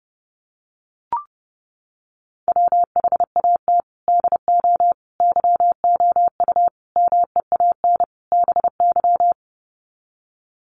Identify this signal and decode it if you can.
Morse code, heard as a keyed tone, about 28 wpm, 700 Hz. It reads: WHAT DO YOU MEAN BY
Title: Sets of 5 Words 30wpm